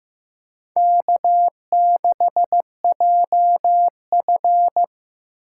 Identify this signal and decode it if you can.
Morse code, heard as a keyed tone, about 15 wpm, 700 Hz. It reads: K6JF
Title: USA Call Signs - 4 Characters - 8x Character Spacing - 15wpm